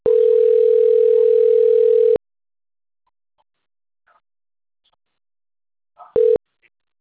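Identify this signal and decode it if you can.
Ringback tone (North America), 440 + 480 Hz.
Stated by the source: Outgoing call recording from the TLMK - Rio de janeiro 2 BKP phone system.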